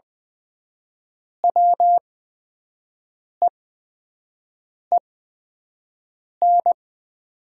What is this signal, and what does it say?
Morse code, heard as a keyed tone, about 20 wpm, spaced as for 5 wpm, 700 Hz. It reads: WEEN
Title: 8x Character Spacing - Top 500 Words - 20wpm